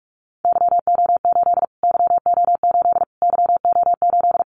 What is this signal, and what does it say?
Morse code, heard as a keyed tone, about 44 wpm, 700 Hz. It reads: YO8 YO8 YO8
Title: Callsign Prefixes for Top 25 DXCC - Repeated 3x - 45wpm